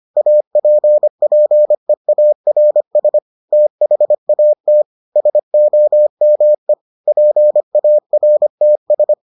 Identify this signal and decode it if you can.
Morse code, heard as a keyed tone, about 25 wpm, 600 Hz. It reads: APPEARS THAT SOME PARTS